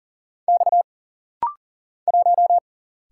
Morse code, heard as a keyed tone, about 40 wpm, 700 Hz. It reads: X 1
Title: Single Letter-Number - Rapid-Fire 40wpm